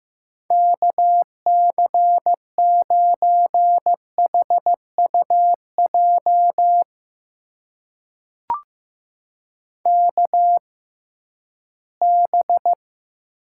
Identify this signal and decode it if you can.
Morse code, heard as a keyed tone, about 15 wpm, spaced as for 11 wpm, 700 Hz. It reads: KC9HUJ KB
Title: USA Call Signs - 6 Characters - 6x Character Spacing - 15wpm